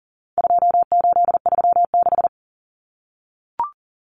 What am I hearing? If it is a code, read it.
Morse code, heard as a keyed tone, about 40 wpm, 700 Hz. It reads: 2836
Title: Common Cut Numbers - 4-Digit 40wpm